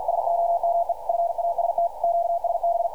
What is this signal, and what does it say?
Morse code, heard as a keyed tone, about 52 wpm, 680 Hz. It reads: SOME FAST QSB
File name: archived / CRSnet